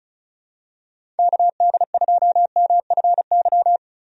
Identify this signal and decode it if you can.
Morse code, heard as a keyed tone, about 36 wpm, 700 Hz. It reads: KD2MFY
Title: USA Call Signs - 6 Characters - 35wpm